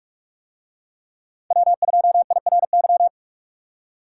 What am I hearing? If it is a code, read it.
Morse code, heard as a keyed tone, about 46 wpm, 700 Hz. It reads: W2IRY